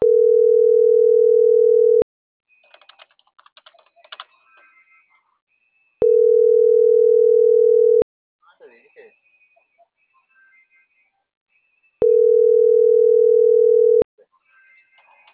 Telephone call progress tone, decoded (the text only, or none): ringback tone